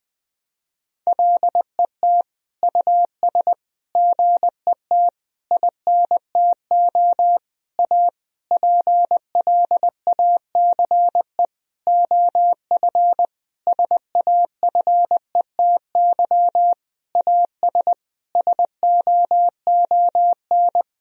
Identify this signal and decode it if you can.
Morse code, heard as a keyed tone, about 20 wpm, 700 Hz. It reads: LET US GET INTO A PLACE OF SAFETY AS SOON
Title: Sentences from Top 1000 Words - Review Part 4 - 20wpm